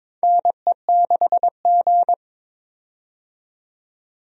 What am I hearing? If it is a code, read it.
Morse code, heard as a keyed tone, about 22 wpm, 700 Hz. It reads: NE6G
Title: USA Call Signs - 4 Characters - 22wpm